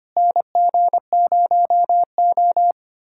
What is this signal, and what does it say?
Morse code, heard as a keyed tone, about 25 wpm, 700 Hz. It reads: NG0O
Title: USA Callsign DE Callsign - 4 Characters - 25wpm